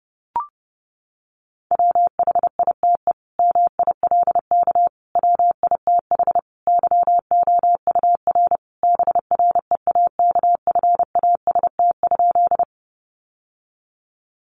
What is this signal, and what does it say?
Morse code, heard as a keyed tone, about 30 wpm, 700 Hz. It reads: WHITE MILK WITH YOUR BREAKFAST?
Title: Sentences from Top 1500 Words 30wpm